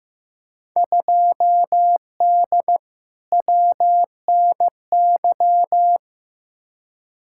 Morse code, heard as a keyed tone, about 15 wpm, 700 Hz. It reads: 2D WNY
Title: Field Day Exchanges - 15wpm